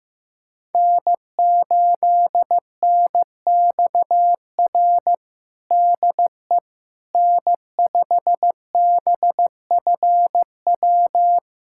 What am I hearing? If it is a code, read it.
Morse code, heard as a keyed tone, about 15 wpm, 700 Hz. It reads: N8NXR DE N5BFW
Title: USA Callsign DE Callsign - 5 Characters - 15wpm